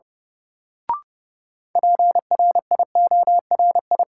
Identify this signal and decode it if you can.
Morse code, heard as a keyed tone, about 32 wpm, 700 Hz. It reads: PRIORI
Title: Sentences from Top 1600 Words - New Word Drill Alt - 30wpm